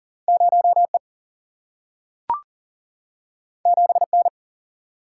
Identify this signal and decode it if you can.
Morse code, heard as a keyed tone, about 41 wpm, 700 Hz. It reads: 0E 7N
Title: Single-Letter Single-Number 40wpm